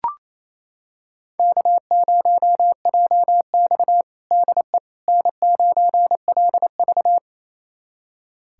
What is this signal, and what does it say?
Morse code, heard as a keyed tone, about 28 wpm, 700 Hz. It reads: K0JX DE N9LV
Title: USA Callsign DE Callsign - 4 Characters - 28wpm